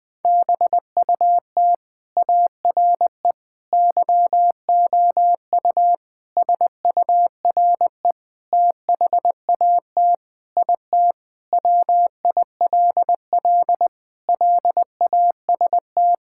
Morse code, heard as a keyed tone, about 21 wpm, 700 Hz. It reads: BUT ARE YOU SURE THAT IT WILL LAST
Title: Sentences from Top 1600 Words 20wpm